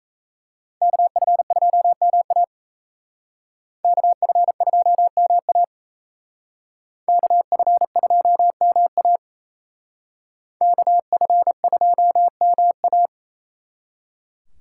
Morse code, heard as a keyed tone, about 35 wpm, 700 Hz. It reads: KF2MA KF2MA KF2MA KF2MA